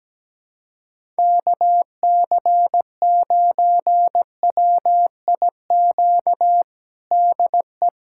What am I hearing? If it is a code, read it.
Morse code, heard as a keyed tone, about 17 wpm, 700 Hz. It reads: KC9WIQ DE
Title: USA Callsign DE Callsign - 6 Characters - 17wpm